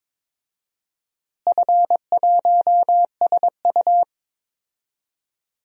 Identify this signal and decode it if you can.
Morse code, heard as a keyed tone, about 22 wpm, 700 Hz. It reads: F1SU
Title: DXCC-100 Call Signs - 4 Characters - Repeated 2x - 22wpm